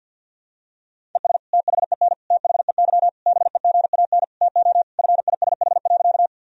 Morse code, heard as a keyed tone, about 51 wpm, 700 Hz. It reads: ES THEN THEY BEGAN TO FISH.
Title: Sentences from Top 300 Words and Abbreviations 50wpm